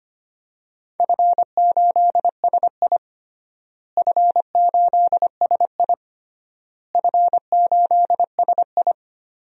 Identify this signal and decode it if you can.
Morse code, heard as a keyed tone, about 25 wpm, 700 Hz. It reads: F8SI F8SI F8SI
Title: DXCC-100 Call Signs - 4 Characters - Repeated 3x - 25wpm